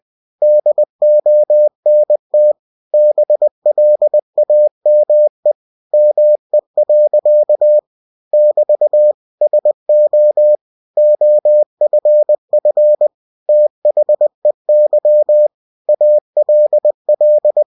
Morse code, heard as a keyed tone, about 20 wpm, 600 Hz. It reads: DONT BLAME ME. <BT> SO OFF THEY ALL
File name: winnie-20wpm